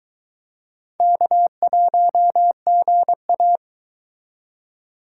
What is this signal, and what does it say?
Morse code, heard as a keyed tone, about 22 wpm, 700 Hz. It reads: K1GA